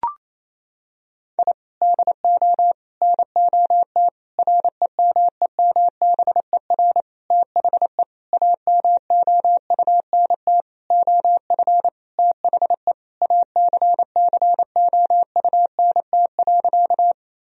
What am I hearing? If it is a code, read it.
Morse code, heard as a keyed tone, about 28 wpm, 700 Hz. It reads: I DO NOT REMEMBER THE AMOUNT OF THE ACCOUNT.
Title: Sentences from Top 1000 Words - Review Part 4 - 28wpm